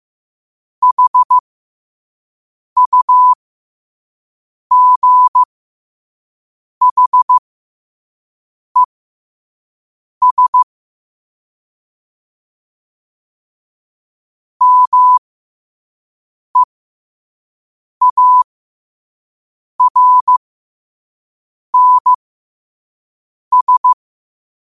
Morse code, heard as a keyed tone, about 15 wpm, spaced as for 5 wpm, 1000 Hz. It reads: HUGHES MEARNS
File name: qotdfiles / 5wpm